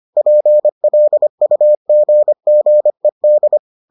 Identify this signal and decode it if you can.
Morse code, heard as a keyed tone, about 25 wpm, 600 Hz. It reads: PLUGGED